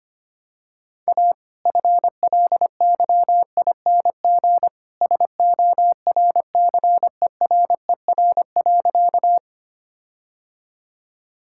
Morse code, heard as a keyed tone, about 25 wpm, 700 Hz. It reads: A FLYING SORCERER.
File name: practice / 25wpm / All-Age Appropriate Jokes - 25wpm